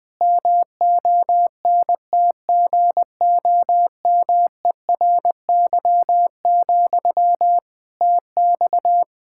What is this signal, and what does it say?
Morse code, heard as a keyed tone, about 20 wpm, 700 Hz. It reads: MONTGOMERY, TX
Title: K1USN SST QSO Practice 20wpm